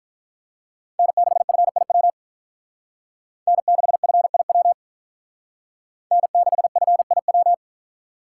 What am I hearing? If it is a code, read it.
Morse code, heard as a keyed tone, about 46 wpm, 700 Hz. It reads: N6FIW N6FIW N6FIW